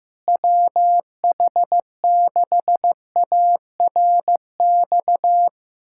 Morse code, heard as a keyed tone, about 15 wpm, 700 Hz. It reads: WH6ARX
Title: USA Callsign DE Callsign - 6 Characters - 15wpm